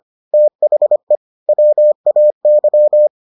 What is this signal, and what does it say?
Morse code, heard as a keyed tone, about 25 wpm, 600 Hz. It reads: THE WAY